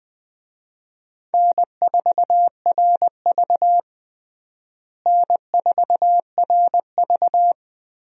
Morse code, heard as a keyed tone, about 20 wpm, 700 Hz. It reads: N4RV N4RV
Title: USA Call Signs - 4 Characters - Repeated 3x - 20wpm